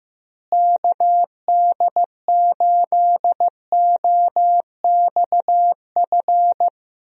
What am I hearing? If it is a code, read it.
Morse code, heard as a keyed tone, about 16 wpm, 700 Hz. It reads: KD8OXF
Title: USA Call Signs - 6 Characters - Repeated 3x - 15wpm